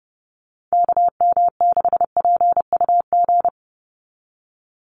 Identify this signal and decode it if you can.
Morse code, heard as a keyed tone, about 30 wpm, 700 Hz. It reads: KM6PUG
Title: USA Call Signs 30wpm